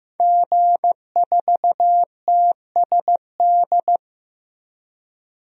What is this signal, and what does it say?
Morse code, heard as a keyed tone, about 15 wpm, 700 Hz. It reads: G4TSD
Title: DXCC-100 Call Signs - 5 Characters - 15wpm